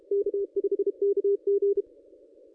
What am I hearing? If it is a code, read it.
Morse code, heard as a keyed tone, about 33 wpm, 440 Hz. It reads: K5KG
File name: contests / ss-cw-k5tr-op-n5rz -1200